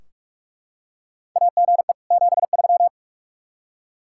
Morse code, heard as a keyed tone, about 44 wpm, 700 Hz. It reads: AGE 73
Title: K1USN SST QSO Practice 45wpm